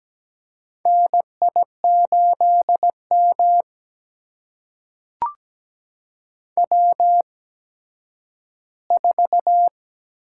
Morse code, heard as a keyed tone, about 17 wpm, spaced as for 10 wpm, 700 Hz. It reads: NI8M W4
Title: USA Call Signs - 4 Characters - 8x Character Spacing - 17wpm